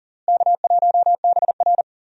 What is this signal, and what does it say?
Morse code, heard as a keyed tone, about 40 wpm, 700 Hz. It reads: K1BR
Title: USA Call Signs - 4 Characters - Repeated 3x - 40wpm